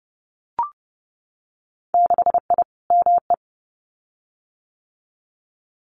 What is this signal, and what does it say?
Morse code, heard as a keyed tone, about 30 wpm, 700 Hz. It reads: 6I ME
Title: Winter Field Day Exchanges Encore 30wpm